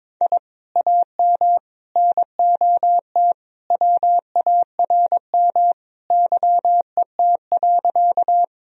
Morse code, heard as a keyed tone, about 22 wpm, 700 Hz. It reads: I AM NOT WARM YET.
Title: Sentences from Top 1000 Words - Review Part 2 - 22wpm